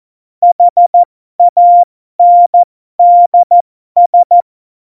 Morse code, heard as a keyed tone, about 14 wpm, spaced as for 12 wpm, 700 Hz. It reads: HANDS
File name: Morsekurs / Witze